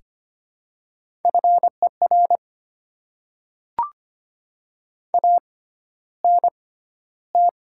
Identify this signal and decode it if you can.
Morse code, heard as a keyed tone, about 25 wpm, spaced as for 13 wpm, 700 Hz. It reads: FER ANT
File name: Top 20 QSO Elements - 6x Character Spacing - 25wpm